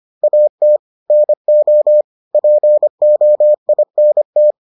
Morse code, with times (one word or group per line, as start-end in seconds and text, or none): 0.2-0.8 AT
1.1-2.0 NO
2.3-4.5 POINT